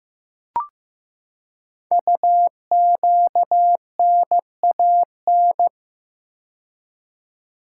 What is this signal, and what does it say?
Morse code, heard as a keyed tone, about 15 wpm, 700 Hz. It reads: UQNAN
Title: Code Groups - 5 Letters - 15wpm